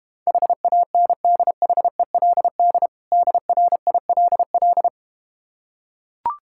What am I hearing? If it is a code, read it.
Morse code, heard as a keyed tone, about 33 wpm, 700 Hz. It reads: HANDHELD DRILL